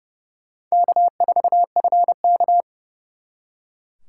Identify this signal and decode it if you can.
Morse code, heard as a keyed tone, about 34 wpm, 700 Hz. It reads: K4FK
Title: Speed Racing - USA 1x2 Call Signs 30wpm